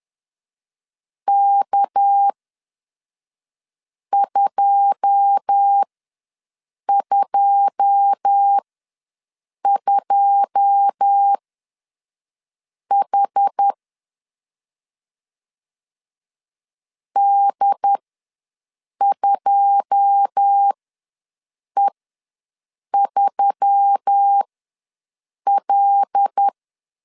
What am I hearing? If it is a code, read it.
Morse code, heard as a keyed tone, about 10 wpm, spaced as for 5 wpm, 800 Hz. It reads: K222H D2E3L
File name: LEZIONI / 9 2 ws 10wpm 4130